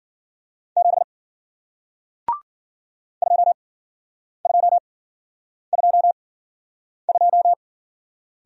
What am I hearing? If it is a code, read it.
Morse code, heard as a keyed tone, about 49 wpm, spaced as for 9 wpm, 700 Hz. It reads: 7 2222
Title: Speed-Racing Single Letter-Number 40wpm